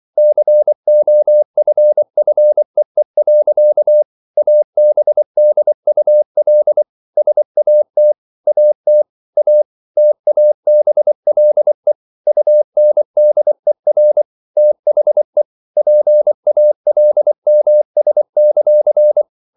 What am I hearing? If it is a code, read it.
Morse code, heard as a keyed tone, about 24 wpm, 600 Hz. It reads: COFFEE. ABDUL SAT AT A TABLE UNDER THE PALMS;